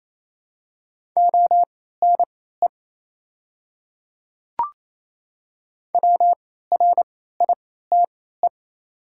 Morse code, heard as a keyed tone, about 28 wpm, spaced as for 16 wpm, 700 Hz. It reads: ONE WRITE
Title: 3x Character Spacing - Top 500 Words - 28wpm